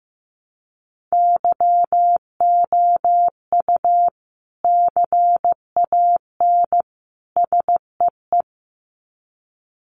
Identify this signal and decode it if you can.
Morse code, heard as a keyed tone, about 15 wpm, 700 Hz. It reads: YOU CAN SEE
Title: Sets of 3 Words 15wpm